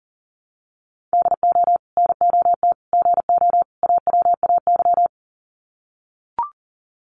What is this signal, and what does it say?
Morse code, heard as a keyed tone, about 40 wpm, 700 Hz. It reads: DO NOT GO AWAY
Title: Sets of 4 Words 40wpm